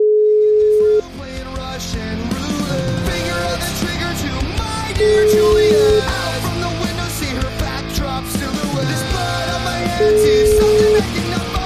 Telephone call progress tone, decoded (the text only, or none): ringback tone